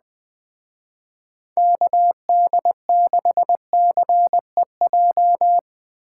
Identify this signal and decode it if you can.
Morse code, heard as a keyed tone, about 20 wpm, 700 Hz. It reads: KD6CEJ
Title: Speed Racing - USA 2x3 Call Signs 17wpm